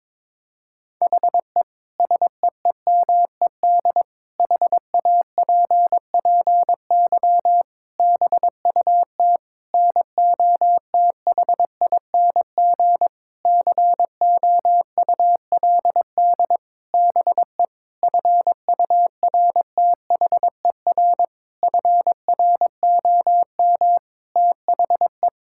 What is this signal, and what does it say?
Morse code, heard as a keyed tone, about 22 wpm, 700 Hz. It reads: HE SEEMED HAPPY BUT NOTHING COULD BE FURTHER FROM THE
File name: practice / 22wpm / Sentences with 8-Word Phrases - 22wpm